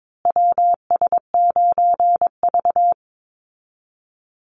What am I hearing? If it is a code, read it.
Morse code, heard as a keyed tone, about 22 wpm, 700 Hz. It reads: WS9V
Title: Call Signs Repeated 2x - 22wpm